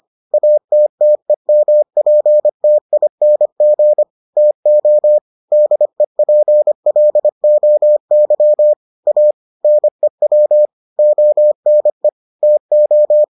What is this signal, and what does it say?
Morse code, heard as a keyed tone, about 25 wpm, 600 Hz. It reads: ATTEMPTING TO DEPLOY A NEW ONE TO